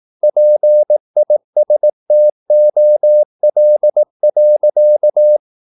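Morse code, heard as a keyed tone, about 18 wpm, 600 Hz. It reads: PISTOL.
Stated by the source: Morse practice at 18 wpm, English text, from CW med Gnister.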